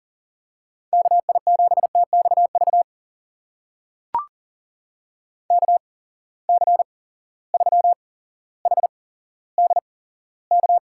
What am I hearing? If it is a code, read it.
Morse code, heard as a keyed tone, about 42 wpm, spaced as for 17 wpm, 700 Hz. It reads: KI7TXV KC3HDK
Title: USA Call Signs - 6 Characters - 8x Character Spacing - 40wpm